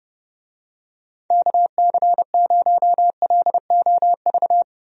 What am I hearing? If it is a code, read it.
Morse code, heard as a keyed tone, about 30 wpm, 700 Hz. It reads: KC0LOV
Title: USA Call Signs - 6 Characters - 30wpm